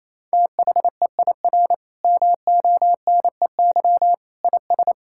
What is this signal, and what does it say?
Morse code, heard as a keyed tone, about 28 wpm, 700 Hz. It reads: THEIR MONEY IS
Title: Sentences from Top 1600 Words 28wpm